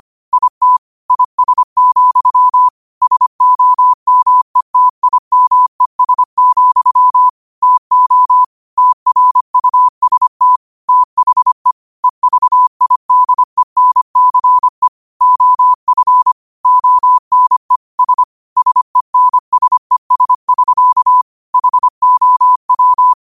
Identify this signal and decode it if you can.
Morse code, heard as a keyed tone, about 25 wpm, 1000 Hz. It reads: IT IS, SOMETIMES, TO TRUST THE EVIDENCE OF ONE S SENSES<SK> HOW
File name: qotdfiles / 25wpm